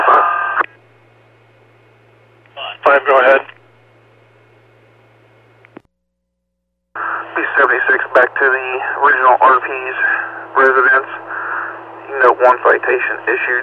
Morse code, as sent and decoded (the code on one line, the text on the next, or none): none